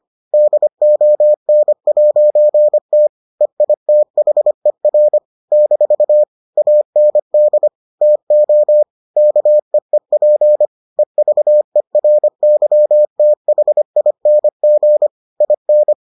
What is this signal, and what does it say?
Morse code, heard as a keyed tone, about 25 wpm, 600 Hz. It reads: DON'T EITHER - AND TO KEEP EVERYTHING IN